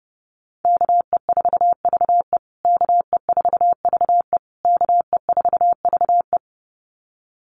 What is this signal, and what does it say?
Morse code, heard as a keyed tone, about 28 wpm, 700 Hz. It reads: KE4VE KE4VE KE4VE
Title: USA Call Signs Repeated 3x 30wpm